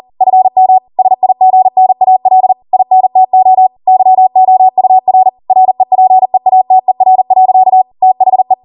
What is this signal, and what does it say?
Morse code, heard as a keyed tone, about 40 wpm, 750 Hz. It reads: FM SIGNAL INTO YOUR REPEATER. THE
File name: Morse / 40WPM